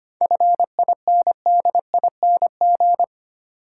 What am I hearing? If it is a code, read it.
Morse code, heard as a keyed tone, about 25 wpm, 700 Hz. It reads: FINDING
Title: Sentences from Top 1100 Words - New Word Drill Alt - 25wpm